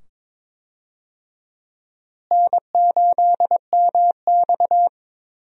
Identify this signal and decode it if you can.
Morse code, heard as a keyed tone, about 22 wpm, 700 Hz. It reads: N8MX
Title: USA Call Signs - 4 Characters - Repeated 2x - 22wpm